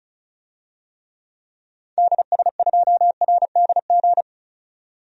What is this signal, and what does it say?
Morse code, heard as a keyed tone, about 36 wpm, 700 Hz. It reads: DS2RDG
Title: DXCC-100 Call Signs - 6 Characters - 35wpm